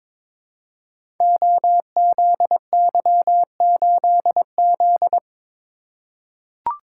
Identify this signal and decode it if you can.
Morse code, heard as a keyed tone, about 23 wpm, 690 Hz. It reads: OZY8Z
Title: Code Groups - 5 Letter:Numbers - 22wpm